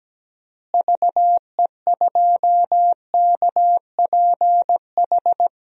Morse code, heard as a keyed tone, about 17 wpm, 700 Hz. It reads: VE2KPH
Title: DXCC-100 Call Signs - 6 Characters - Repeated 2x - 17wpm